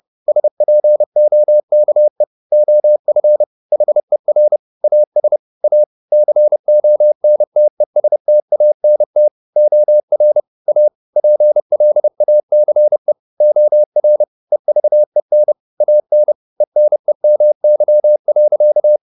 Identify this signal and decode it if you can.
Morse code, heard as a keyed tone, about 30 wpm, 600 Hz. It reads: SPOKE OF HER AS A CONTESTANT OR A PLACE OR EVEN AN ENEMY.
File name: the old man and the sea / the old man and the sea 1 30wpm